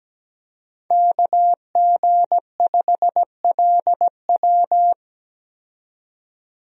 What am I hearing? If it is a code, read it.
Morse code, heard as a keyed tone, about 17 wpm, 700 Hz. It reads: KG5LW
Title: USA Call Signs - 5 Characters - 17wpm